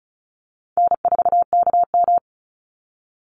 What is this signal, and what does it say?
Morse code, heard as a keyed tone, about 34 wpm, 700 Hz. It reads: N4KM
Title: Easy Call Signs Repeated 2x - 35wpm